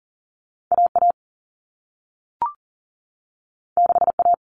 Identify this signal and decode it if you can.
Morse code, heard as a keyed tone, about 38 wpm, 700 Hz. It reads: AA 6A
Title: All Cut Numbers - 2-Digit 40wpm